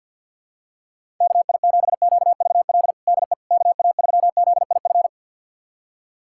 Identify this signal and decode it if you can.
Morse code, heard as a keyed tone, about 51 wpm, 700 Hz. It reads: KI7QVL DE KA3ZIF
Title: USA Callsign DE Callsign 50wpm